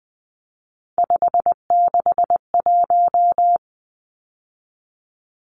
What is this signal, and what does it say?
Morse code, heard as a keyed tone, about 20 wpm, 700 Hz. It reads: 561
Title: Numbers - 3-Digit 20wpm